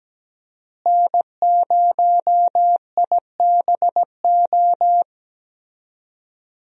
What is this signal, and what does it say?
Morse code, heard as a keyed tone, about 17 wpm, 700 Hz. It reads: N0IBO
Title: USA Call Signs - 5 Characters - Repeated 3x - 17wpm